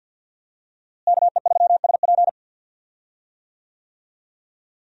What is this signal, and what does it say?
Morse code, heard as a keyed tone, about 52 wpm, 700 Hz. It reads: KE3SP
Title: USA Call Signs - 5 Characters - Repeated 3x - 50wpm